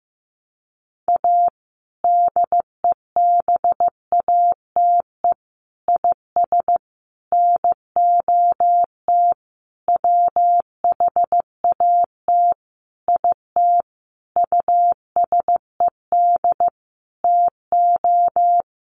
Morse code, heard as a keyed tone, about 15 wpm, 700 Hz. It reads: A DEBATE IS NOT WHAT IT USED TO
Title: Sentences from Top 1100 Words 15wpm